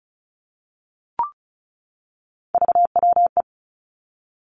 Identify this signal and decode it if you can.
Morse code, heard as a keyed tone, about 36 wpm, 700 Hz. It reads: VWE
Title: Three Challenging Letter-Number Combinations 35wpm